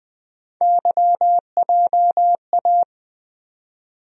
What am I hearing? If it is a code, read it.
Morse code, heard as a keyed tone, about 20 wpm, 700 Hz. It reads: YJA
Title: USA Call Sign Postfixes - 3 Letters 20wpm